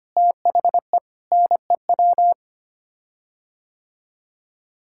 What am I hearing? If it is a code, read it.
Morse code, heard as a keyed tone, about 25 wpm, 700 Hz. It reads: THE NEW
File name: Sets of 9 Words - 25wpm